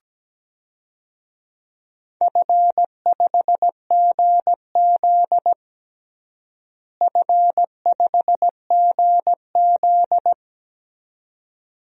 Morse code, heard as a keyed tone, about 17 wpm, 700 Hz. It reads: F5GZ F5GZ
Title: DXCC-100 Call Signs - 4 Characters - Repeated 2x - 17wpm